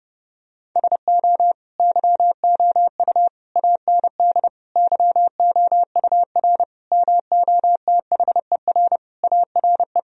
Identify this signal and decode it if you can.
Morse code, heard as a keyed tone, about 30 wpm, 700 Hz. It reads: SO YOU AND YOUR MOTHER ARE
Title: Sentences from Top 1000 Words - Review Part 4 - 30wpm